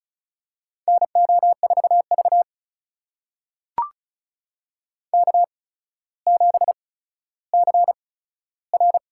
Morse code, heard as a keyed tone, about 36 wpm, spaced as for 15 wpm, 700 Hz. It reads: NO4V K7CR
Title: USA Call Signs - 4 Characters - 8x Character Spacing - 35wpm